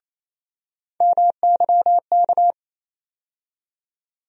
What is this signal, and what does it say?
Morse code, heard as a keyed tone, about 28 wpm, 700 Hz. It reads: MYK
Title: USA Call Sign Postfixes - 3 Letters 28wpm